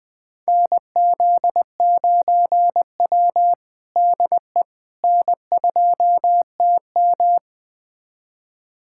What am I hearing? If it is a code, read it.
Morse code, heard as a keyed tone, about 21 wpm, 700 Hz. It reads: NZ9W DE N2TM
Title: USA Callsign DE Callsign - 4 Characters - 20wpm